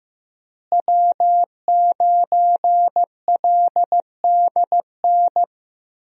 Morse code, heard as a keyed tone, about 16 wpm, 700 Hz. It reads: W9LDN
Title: USA Call Signs - 5 Characters - Repeated 3x - 15wpm